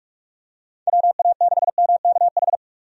Morse code, heard as a keyed tone, about 46 wpm, 700 Hz. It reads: WA6MKH
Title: USA Call Signs - 6 Characters - 45wpm